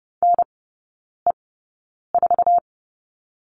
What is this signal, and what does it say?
Morse code, heard as a keyed tone, about 30 wpm, spaced as for 9 wpm, 700 Hz. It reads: NE4
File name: 081 Lesson 14e - Number 2 - Three Characters 30wpm